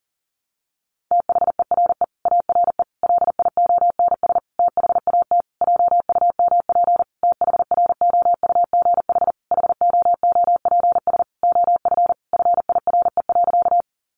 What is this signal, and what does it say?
Morse code, heard as a keyed tone, about 40 wpm, 700 Hz. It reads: THERE ARE LIONS THAT JUMP THROUGH HOOPS OF FIRE.
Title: Easy Phrases 40wpm